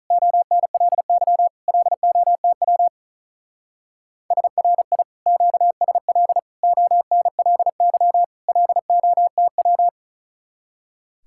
Morse code, heard as a keyed tone, about 44 wpm, 700 Hz. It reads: ONLY LOTW SRI QSL ONLY LOTW